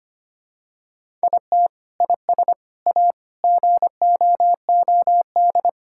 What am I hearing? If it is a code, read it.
Morse code, heard as a keyed tone, about 25 wpm, 700 Hz. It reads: IT IS A GOOD
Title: Sentences from Top 1000 Words - Review Part 5 - 25wpm